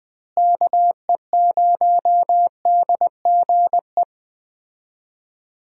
Morse code, heard as a keyed tone, about 20 wpm, 700 Hz. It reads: KE0DGE
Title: USA Call Signs - 6 Characters - Repeated 2x - 20wpm